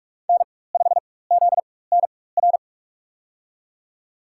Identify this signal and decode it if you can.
Morse code, heard as a keyed tone, about 44 wpm, spaced as for 21 wpm, 690 Hz. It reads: N5ZNR
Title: USA Call Signs - 4x Character Spacing - 45wpm